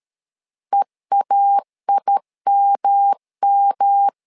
Morse code, heard as a keyed tone, about 13 wpm, 800 Hz. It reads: EAIMM
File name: LEZIONI / 3 MWRO / 3 MW 12wpm 1633